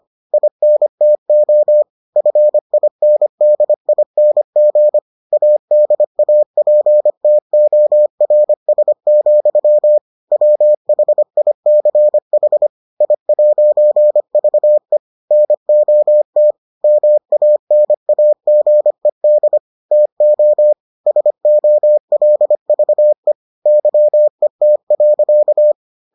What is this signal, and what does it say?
Morse code, heard as a keyed tone, about 25 wpm, 600 Hz. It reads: INTO FINDING ADAPTORS, WHICH I'VE NOT MANAGED TO SOLVE YET.